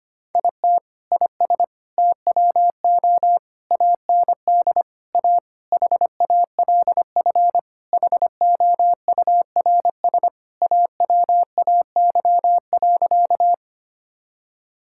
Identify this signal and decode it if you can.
Morse code, heard as a keyed tone, about 25 wpm, 700 Hz. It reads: IT IS TWO AND A HALF HOURS AWAY.
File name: practice / 25wpm / Sentences with 4-Word Phrases - 25wpm